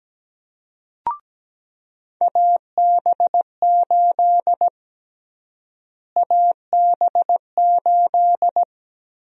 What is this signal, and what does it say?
Morse code, heard as a keyed tone, about 17 wpm, 700 Hz. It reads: AB8 AB8
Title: USA Call Sign Prefixes - Repeated 2x - 17wpm